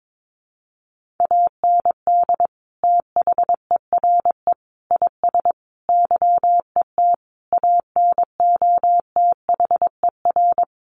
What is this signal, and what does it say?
Morse code, heard as a keyed tone, about 22 wpm, 700 Hz. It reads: AND THERE IS YET ANOTHER
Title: Sentences from Top 1500 Words 22wpm